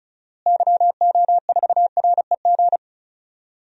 Morse code, heard as a keyed tone, about 36 wpm, 700 Hz. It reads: YO4REG
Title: DXCC-100 Call Signs - 6 Characters - Repeated 3x - 35wpm